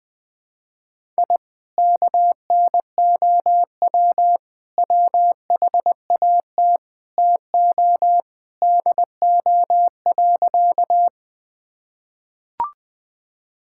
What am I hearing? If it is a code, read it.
Morse code, heard as a keyed tone, about 21 wpm, 700 Hz. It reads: I KNOW WHAT TO DO.
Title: Sentences with 3-Word Phrases - 20wpm